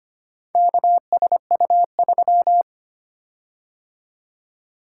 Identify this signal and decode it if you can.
Morse code, heard as a keyed tone, about 25 wpm, 700 Hz. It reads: KSU3
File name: practice / 25wpm / Four Letter-Number Combinations - 25wpm